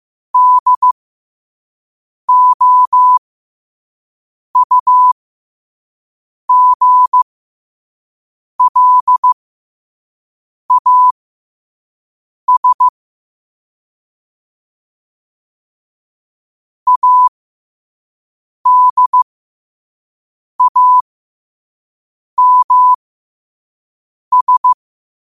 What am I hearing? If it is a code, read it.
Morse code, heard as a keyed tone, about 15 wpm, spaced as for 5 wpm, 1000 Hz. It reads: DOUGLAS ADAMS
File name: qotdfiles / 5wpm